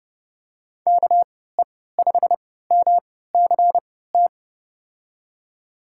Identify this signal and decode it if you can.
Morse code, heard as a keyed tone, about 30 wpm, spaced as for 17 wpm, 700 Hz. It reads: KE5MCT
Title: USA Call Signs - 6 Characters - 3x Character Spacing - 30wpm